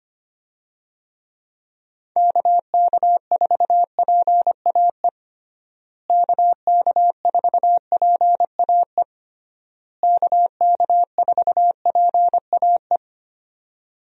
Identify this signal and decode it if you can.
Morse code, heard as a keyed tone, about 25 wpm, 700 Hz. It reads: KK4PAE KK4PAE KK4PAE
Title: USA Call Signs - 6 Characters - Repeated 3x - 25wpm